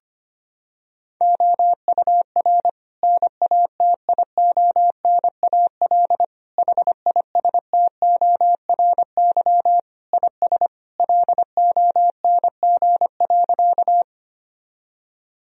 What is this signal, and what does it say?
Morse code, heard as a keyed tone, about 25 wpm, 700 Hz. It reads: OUR NATIONAL HISTORY IS LONG.
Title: Sentences from Top 1000 Words - Review Part 4 - 25wpm